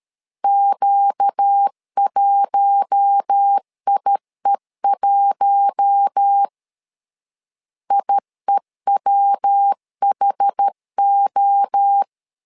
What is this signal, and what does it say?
Morse code, heard as a keyed tone, about 13 wpm, 800 Hz. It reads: Q1IE1 IEWHO
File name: LEZIONI / 4 Q1 12wpm 2028